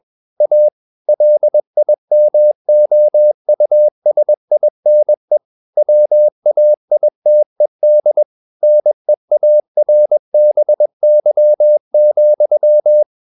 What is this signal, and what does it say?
Morse code, heard as a keyed tone, about 21 wpm, 600 Hz. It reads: A LIMOUSINE WAITED NEARBY,
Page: CW med Gnister